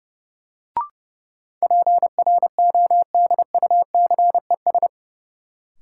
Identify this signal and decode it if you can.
Morse code, heard as a keyed tone, about 30 wpm, 700 Hz. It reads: PRODUCES
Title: Sentences from Top 1000 Words - Review Part 6 - New Word Drill - 30wpm